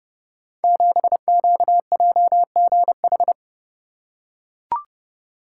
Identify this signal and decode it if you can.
Morse code, heard as a keyed tone, about 30 wpm, 700 Hz. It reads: 7QJGH
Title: Code Groups - 5 Letter:Numbers - 30wpm